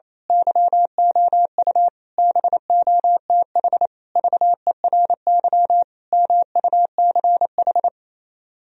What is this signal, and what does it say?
Morse code, heard as a keyed tone, about 27 wpm, 700 Hz. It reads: YOU BOTH VERY MUCH
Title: Speed-Racing - Sets of 5 Words 28wpm